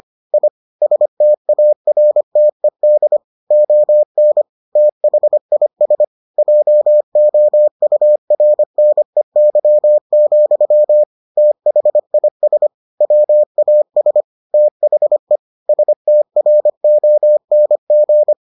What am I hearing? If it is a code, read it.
Morse code, heard as a keyed tone, about 25 wpm, 600 Hz. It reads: I STARTED ON THIS JOURNEY, THIS WAS THE STRONG